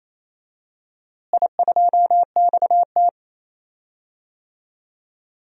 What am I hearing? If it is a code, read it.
Morse code, heard as a keyed tone, about 28 wpm, 700 Hz. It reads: I2XT
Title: DXCC-100 Call Signs - 4 Characters - Repeated 2x - 28wpm